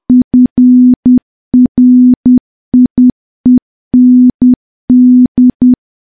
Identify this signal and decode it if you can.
Morse code, heard as a keyed tone, about 10 wpm, 250 Hz. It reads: FRIEND
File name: morse / morse-training-10WPM